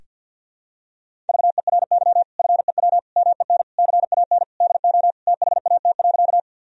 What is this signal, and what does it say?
Morse code, heard as a keyed tone, about 50 wpm, 700 Hz. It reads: VERY FEW MEN CAN DO THAT.
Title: Sentences from Top 1000 Words - Review Part 1 - 50wpm